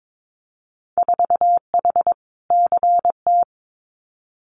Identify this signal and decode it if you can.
Morse code, heard as a keyed tone, about 22 wpm, 700 Hz. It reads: 4H CT
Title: Winter Field Day Exchanges Encore 22wpm